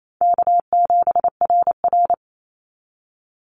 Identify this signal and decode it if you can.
Morse code, heard as a keyed tone, about 28 wpm, 700 Hz. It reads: K7RR
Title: Call Signs Repeated 2x - 28wpm